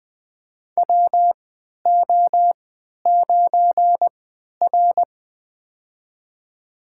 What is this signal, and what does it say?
Morse code, heard as a keyed tone, about 20 wpm, spaced as for 11 wpm, 700 Hz. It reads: WO9R
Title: USA Call Signs - 4 Characters - 3x Character Spacing - 20wpm